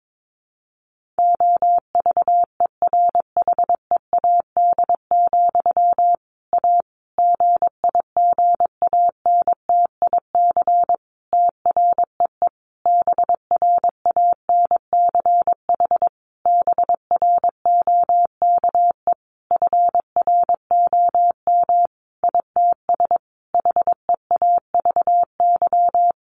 Morse code, heard as a keyed tone, about 22 wpm, 700 Hz. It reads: OVERHEAD, A GIGANTIC TREE BRANCH BROKE FROM ITS HEAVY
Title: A Sound Of Thunder 22wpm